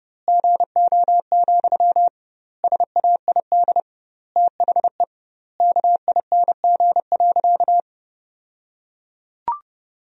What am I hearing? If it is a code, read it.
Morse code, heard as a keyed tone, about 32 wpm, 700 Hz. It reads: GO, SAID THE KING.
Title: 2x Word-Spacing - Sentences from Top 400 Words - 30wpm